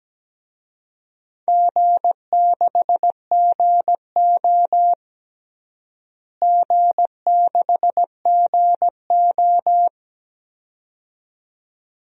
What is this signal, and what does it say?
Morse code, heard as a keyed tone, about 17 wpm, 700 Hz. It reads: G6GO G6GO
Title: DXCC-100 Call Signs - 4 Characters - Repeated 2x - 17wpm